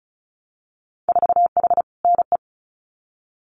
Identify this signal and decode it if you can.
Morse code, heard as a keyed tone, about 36 wpm, 700 Hz. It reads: 4H NE
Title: Winter Field Day Exchanges Encore 35wpm